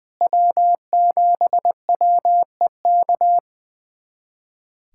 Morse code, heard as a keyed tone, about 20 wpm, 700 Hz. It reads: W7WEK
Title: USA Callsign DE Callsign - 5 Characters - 20wpm